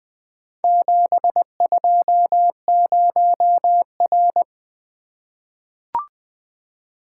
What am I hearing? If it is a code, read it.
Morse code, heard as a keyed tone, about 20 wpm, 700 Hz. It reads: 720R
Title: Four Letter-Number Combinations - 20wpm